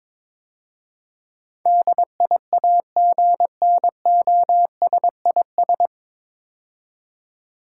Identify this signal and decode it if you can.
Morse code, heard as a keyed tone, about 22 wpm, 700 Hz. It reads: DIAGNOSIS
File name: 9 Letter Words - First 3 Letters Given - 22wpm